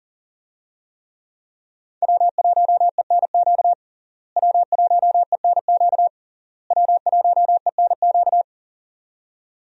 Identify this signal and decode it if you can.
Morse code, heard as a keyed tone, about 40 wpm, 700 Hz. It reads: W1ENQ W1ENQ W1ENQ
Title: USA Call Signs - 5 Characters - Repeated 3x - 40wpm